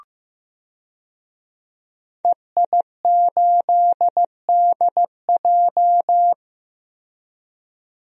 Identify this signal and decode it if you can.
Morse code, heard as a keyed tone, about 15 wpm, 700 Hz. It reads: EI8DJ
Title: DXCC-100 Call Signs - 5 Characters - Repeated 2x - 15wpm